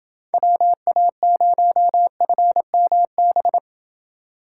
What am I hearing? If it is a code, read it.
Morse code, heard as a keyed tone, about 28 wpm, 700 Hz. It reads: WA0FMB